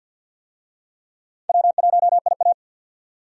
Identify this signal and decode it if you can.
Morse code, heard as a keyed tone, about 51 wpm, 700 Hz. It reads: W1IA
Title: USA Call Signs - 4 Characters - 50wpm